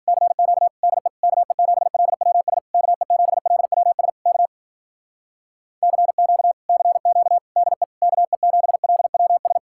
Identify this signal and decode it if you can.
Morse code, heard as a keyed tone, about 51 wpm, 700 Hz. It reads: CQ DE KE7LWS KE7LWS K CQ CQ DE KE7LWS